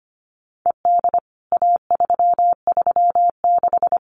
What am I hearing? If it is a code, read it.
Morse code, heard as a keyed tone, about 25 wpm, 700 Hz. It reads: ED A336
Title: CWT Exchange 25wpm